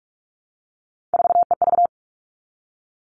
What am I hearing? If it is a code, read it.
Morse code, heard as a keyed tone, about 44 wpm, 700 Hz. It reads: 4EV
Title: All Cut Numbers - 3-Digit 45wpm